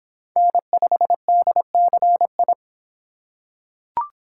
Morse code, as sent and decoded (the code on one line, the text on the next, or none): -. ..... -.. -.-. ..
N5DCI